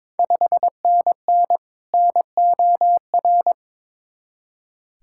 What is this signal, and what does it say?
Morse code, heard as a keyed tone, about 22 wpm, 700 Hz. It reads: 5NN NOR
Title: Ontario QSO Party Exchange 22wpm